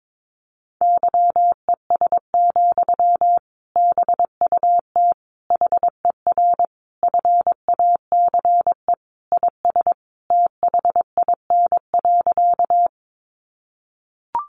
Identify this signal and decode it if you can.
Morse code, heard as a keyed tone, about 22 wpm, 700 Hz. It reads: YES, BUT HER FACE IS THIN.
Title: Sentences from Top 1200 Words 22wpm